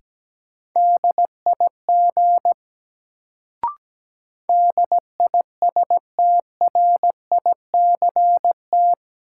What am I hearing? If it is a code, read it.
Morse code, heard as a keyed tone, about 17 wpm, 700 Hz. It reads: DIG DISTRICT
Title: Sentences from Top 1600 Words - New Word Drill Alt - 17wpm